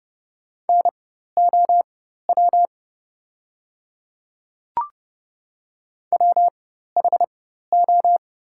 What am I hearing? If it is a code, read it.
Morse code, heard as a keyed tone, about 30 wpm, spaced as for 14 wpm, 700 Hz. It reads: NOW WHO
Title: 4x Character Spacing - Top 100 Words - 30wpm